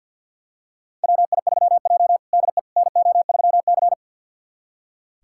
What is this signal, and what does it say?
Morse code, heard as a keyed tone, about 46 wpm, 700 Hz. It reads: WI3J DE NO3C
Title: USA Callsign DE Callsign - 4 Characters - 50wpm